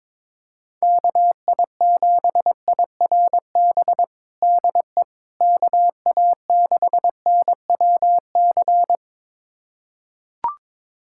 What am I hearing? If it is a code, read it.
Morse code, heard as a keyed tone, about 21 wpm, 700 Hz. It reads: KI7IRB DE KA6NWC
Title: USA Callsign DE Callsign - 6 Characters - 22wpm